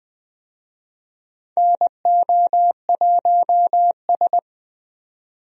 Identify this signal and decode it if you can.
Morse code, heard as a keyed tone, about 20 wpm, 700 Hz. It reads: NO1S
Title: USA Call Signs - 4 Characters - 20wpm